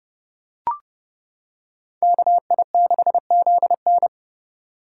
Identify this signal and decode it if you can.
Morse code, heard as a keyed tone, about 28 wpm, 700 Hz. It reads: KI6ZN